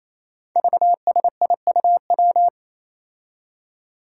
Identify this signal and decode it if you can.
Morse code, heard as a keyed tone, about 27 wpm, 700 Hz. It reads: VSIUW
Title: Five Challenging Letter-Number Combinations 28wpm